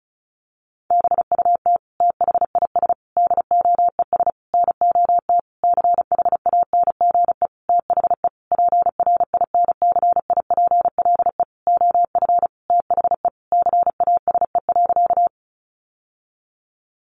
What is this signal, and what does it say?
Morse code, heard as a keyed tone, about 34 wpm, 700 Hz. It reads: BUT THIS DOES NOT CHANGE THE PRINCIPLE OF THE CASE.
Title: Sentences from Top 1400 Words 35wpm